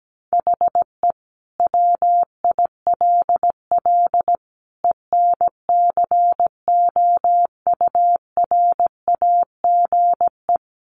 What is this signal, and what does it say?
Morse code, heard as a keyed tone, about 17 wpm, 700 Hz. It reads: HE WILL ENCOURAGE
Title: Sentences from Top 1400 Words 17wpm